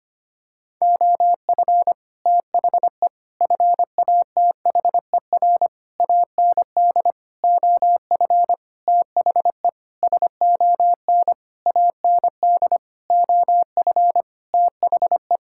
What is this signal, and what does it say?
Morse code, heard as a keyed tone, about 25 wpm, 700 Hz. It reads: OF THE FATHER AND OF THE SON AND OF THE
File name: practice / 25wpm / Sentences with 9-Word Phrases - 25wpm